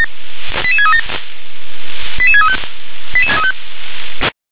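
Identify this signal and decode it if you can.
Five-tone selective call (ZVEI), heard as quick runs of five tones, tones of about 70 ms. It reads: 88637 88635
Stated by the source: Five-tone ZVEI selective call received on an SDR.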